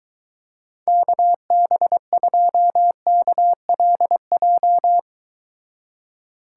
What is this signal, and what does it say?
Morse code, heard as a keyed tone, about 23 wpm, 700 Hz. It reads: KB2KLJ